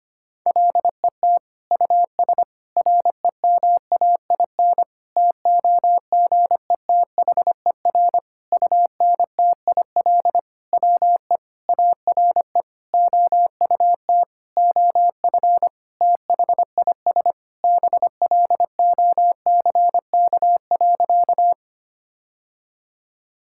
Morse code, heard as a keyed tone, about 25 wpm, 700 Hz. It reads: LET US REMAIN TOGETHER UNTIL WE ARE OUT OF THIS BLOCK.
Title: Sentences from Top 1600 Words 25wpm